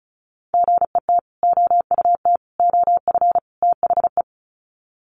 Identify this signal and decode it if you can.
Morse code, heard as a keyed tone, about 34 wpm, 700 Hz. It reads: GET OUT OF THE
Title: Sets of 4 Words 35wpm